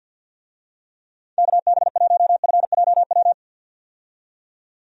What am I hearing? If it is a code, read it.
Morse code, heard as a keyed tone, about 51 wpm, 700 Hz. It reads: KB1FJW
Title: USA Call Signs - 6 Characters - 50wpm